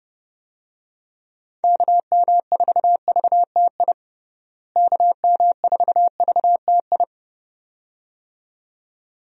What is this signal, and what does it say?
Morse code, heard as a keyed tone, about 30 wpm, 700 Hz. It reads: KM4VTI KM4VTI
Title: USA Call Signs - 6 Characters - Repeated 2x - 30wpm